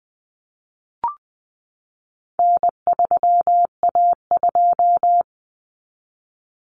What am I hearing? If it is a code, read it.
Morse code, heard as a keyed tone, about 20 wpm, 700 Hz. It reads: N3A2
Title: Common Cut Numbers - 4-Digit 20wpm